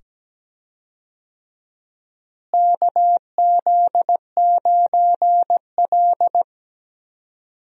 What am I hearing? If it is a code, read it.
Morse code, heard as a keyed tone, about 17 wpm, 700 Hz. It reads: KZ9L
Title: USA Call Signs - 4 Characters - Repeated 2x - 17wpm